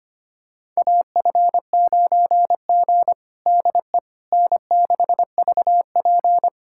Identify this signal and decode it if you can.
Morse code, heard as a keyed tone, about 25 wpm, 700 Hz. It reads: AF9G DE N6VP
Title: USA Callsign DE Callsign - 4 Characters - 25wpm